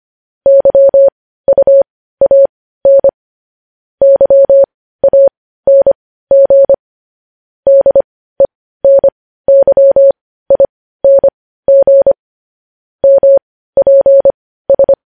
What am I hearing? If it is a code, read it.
Morse code, heard as a keyed tone, about 25 wpm, spaced as for 15 wpm, 550 Hz. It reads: YUAN YANG DENYING MPS
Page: news-headlines-in-morse-code-at-15-wpm